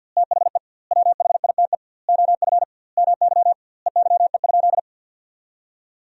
Morse code, heard as a keyed tone, about 50 wpm, 700 Hz. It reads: THE WHITE OF MY EYE?
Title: 2x Word-Spacing - Sentences from Top 500 Words - 50wpm